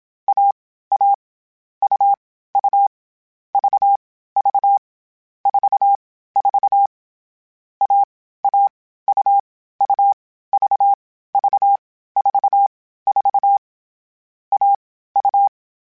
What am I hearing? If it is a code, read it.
Morse code, heard as a keyed tone, about 27 wpm, 800 Hz. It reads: A A U U V V 4 4 A A U U V V 4 4 A U
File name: Short Runs / Short Runs-mp3s / AUV AUV4 @25